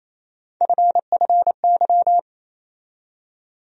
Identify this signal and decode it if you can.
Morse code, heard as a keyed tone, about 27 wpm, 700 Hz. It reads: FFY
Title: USA Call Sign Postfixes - 3 Letters 28wpm